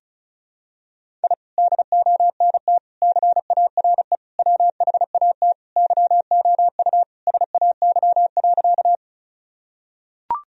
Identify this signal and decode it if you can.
Morse code, heard as a keyed tone, about 36 wpm, 700 Hz. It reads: I DONT CARE WHAT YOU SAY.
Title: Sentences from Top 1000 Words - Review Part 1 - 35wpm